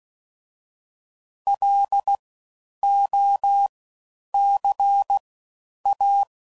Morse code, heard as a keyed tone, about 16 wpm, spaced as for 9 wpm, 800 Hz. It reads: LOCA
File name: Suffixes / Special INGs / Fast Ings